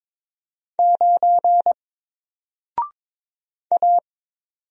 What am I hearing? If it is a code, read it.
Morse code, heard as a keyed tone, about 22 wpm, 700 Hz. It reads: 9 A
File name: Single-Letter-Number-22wpm